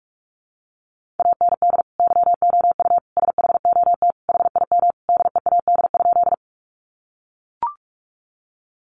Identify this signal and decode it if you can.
Morse code, heard as a keyed tone, about 44 wpm, 700 Hz. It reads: AND YOU SHOT HIM DEAD?
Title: Sentences from Top 1500 Words 45wpm